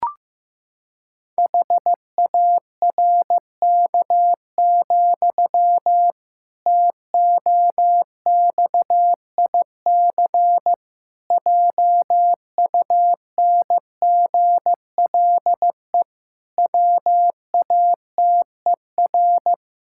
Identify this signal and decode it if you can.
Morse code, heard as a keyed tone, about 16 wpm, 700 Hz. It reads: HARK, TOXIC JUNGLE WATER
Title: Pangram Challenge v2 - 15wpm